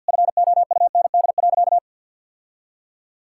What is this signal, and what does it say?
Morse code, heard as a keyed tone, about 50 wpm, 700 Hz. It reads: FOUND.
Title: Sentences from Top 1600 Words 50wpm